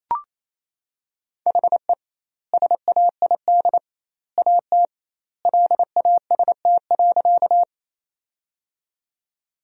Morse code, heard as a keyed tone, about 27 wpm, 700 Hz. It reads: HE SAID AT LAST.
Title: 2x Word-Spacing - Sentences from Top 300 Words - 28wpm